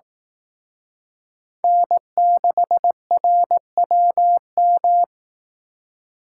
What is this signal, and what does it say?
Morse code, heard as a keyed tone, about 18 wpm, 700 Hz. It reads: N6RWM
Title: Speed Racing - USA 1x3 Call Signs 15wpm